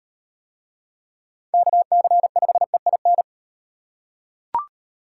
Morse code, heard as a keyed tone, about 38 wpm, 700 Hz. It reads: KC5EIN